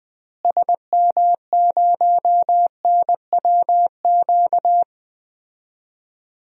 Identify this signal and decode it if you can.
Morse code, heard as a keyed tone, about 20 wpm, 700 Hz. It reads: SM0NWQ
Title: DXCC-100 Call Signs - 6 Characters - Repeated 3x - 20wpm